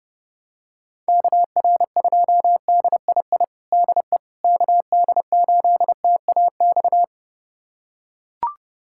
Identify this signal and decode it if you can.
Morse code, heard as a keyed tone, about 29 wpm, 700 Hz. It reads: KR2DII DE KD8TAX
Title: USA Callsign DE Callsign 30wpm